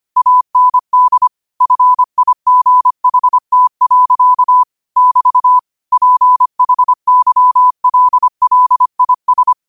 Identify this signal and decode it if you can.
Morse code, heard as a keyed tone, about 25 wpm, 1000 Hz. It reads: AND FIGHT. <BT> PHYLLIS
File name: qotdfiles / 25wpm